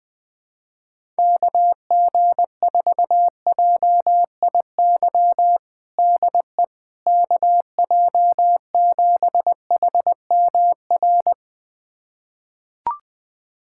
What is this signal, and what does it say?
Morse code, heard as a keyed tone, about 20 wpm, 700 Hz. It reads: KG4JIY DE KJ7HMR
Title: USA Callsign DE Callsign 20wpm